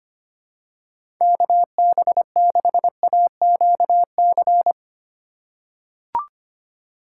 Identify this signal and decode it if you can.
Morse code, heard as a keyed tone, about 25 wpm, 700 Hz. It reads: KB6AQC
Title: USA Call Signs - 6 Characters - 3x Character Spacing - 25wpm